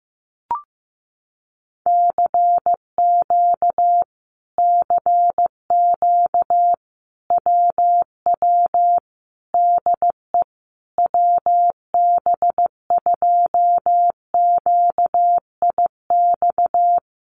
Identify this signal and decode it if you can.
Morse code, heard as a keyed tone, about 15 wpm, 700 Hz. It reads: CQ CQ WW DE WB2QIX
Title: Common QSO Phrases Encore 15wpm